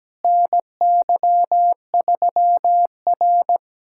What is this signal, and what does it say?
Morse code, heard as a keyed tone, about 17 wpm, 700 Hz. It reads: NY3R
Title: USA Callsign DE Callsign - 4 Characters - 17wpm